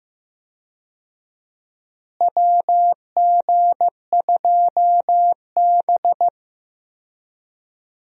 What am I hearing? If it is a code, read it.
Morse code, heard as a keyed tone, about 15 wpm, 700 Hz. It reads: WG2B
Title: USA Call Signs - 4 Characters - Repeated 2x - 15wpm